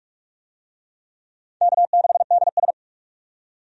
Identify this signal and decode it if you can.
Morse code, heard as a keyed tone, about 46 wpm, 700 Hz. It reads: K6DS
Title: USA Call Signs - 4 Characters - 45wpm